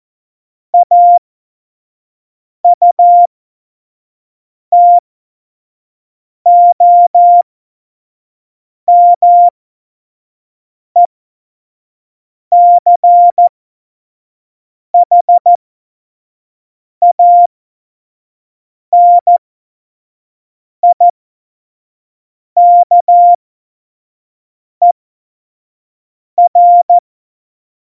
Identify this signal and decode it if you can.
Morse code, heard as a keyed tone, about 14 wpm, spaced as for 5 wpm, 700 Hz. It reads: AUTOMECHANIKER
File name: 25 F70